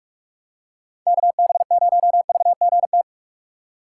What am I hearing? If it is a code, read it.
Morse code, heard as a keyed tone, about 46 wpm, 700 Hz. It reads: KB0VGT
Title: USA Call Signs - 6 Characters - 2x Character Spacing - 45wpm